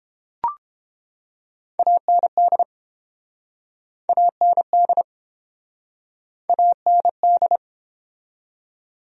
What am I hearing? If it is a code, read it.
Morse code, heard as a keyed tone, about 30 wpm, 700 Hz. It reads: AND AND AND